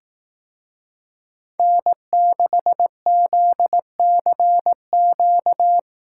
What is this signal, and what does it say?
Morse code, heard as a keyed tone, about 18 wpm, 700 Hz. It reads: N6ZCQ
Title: Speed Racing - USA Call Signs 15wpm